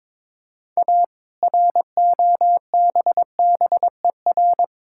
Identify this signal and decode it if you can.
Morse code, heard as a keyed tone, about 22 wpm, 690 Hz. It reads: A ROBBER
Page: morse-code-ninja-practice-22wpm